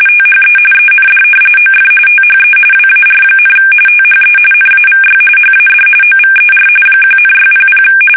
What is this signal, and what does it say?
RTTY, about 45.45 baud. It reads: THE END OF THE WAR, THE NAVY HAD OVER 1,600
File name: fsk850 / usnavy1